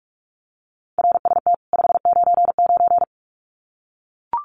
Morse code, heard as a keyed tone, about 44 wpm, 700 Hz. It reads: RST 599
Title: Common QSO Phrases 45wpm